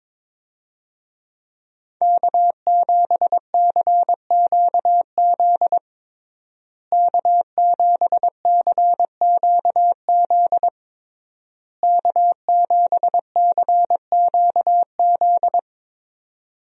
Morse code, heard as a keyed tone, about 21 wpm, 700 Hz. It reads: K7CQZ K7CQZ K7CQZ
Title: USA Call Signs - 5 Characters - Repeated 3x - 22wpm